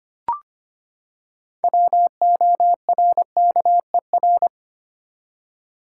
Morse code, heard as a keyed tone, about 25 wpm, 700 Hz. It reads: WORKER
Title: Six Letter Words - 25wpm